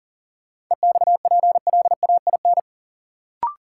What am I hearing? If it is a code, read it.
Morse code, heard as a keyed tone, about 40 wpm, 700 Hz. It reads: EXPLAIN
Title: Sentences from Top 1100 Words - New Word Drill Alt - 40wpm